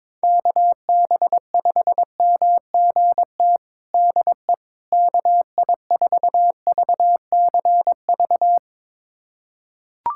Morse code, heard as a keyed tone, about 22 wpm, 700 Hz. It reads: KB5MGT DE KI4VCV
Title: USA Callsign DE Callsign - 6 Characters - 22wpm